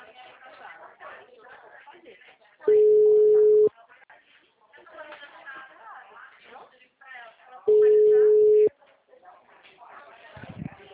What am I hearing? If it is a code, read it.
Ringback tone (Europe, 425 Hz), with speech.
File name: out- -2007- - - .68208